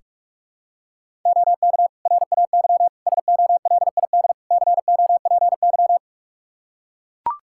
Morse code, heard as a keyed tone, about 41 wpm, 700 Hz. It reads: OK RAY SOLID COPY
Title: Practice QSO - Exchange 3 - More Info - 45wpm